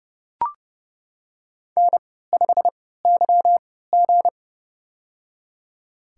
Morse code, heard as a keyed tone, about 29 wpm, spaced as for 17 wpm, 700 Hz. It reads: N5YG
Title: USA Call Signs - 4 Characters - 3x Character Spacing - 30wpm